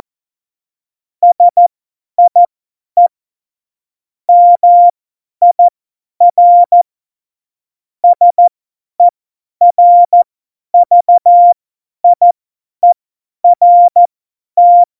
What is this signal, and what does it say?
Morse code, heard as a keyed tone, about 14 wpm, spaced as for 10 wpm, 700 Hz. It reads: SIE MIR SERVIERT
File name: Morsekurs / Witze / Witz 18. Juni 2019, 50 F70